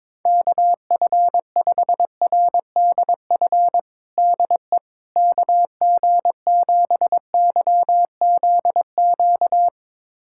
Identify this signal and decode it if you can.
Morse code, heard as a keyed tone, about 22 wpm, 700 Hz. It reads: KF5RDF DE KG7YZQ
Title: USA Callsign DE Callsign - 6 Characters - 22wpm